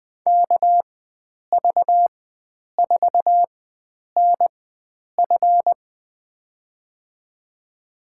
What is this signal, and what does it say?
Morse code, heard as a keyed tone, about 21 wpm, spaced as for 9 wpm, 700 Hz. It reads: KV4NF
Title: USA Call Signs - 5 Characters - 4x Character Spacing - 20wpm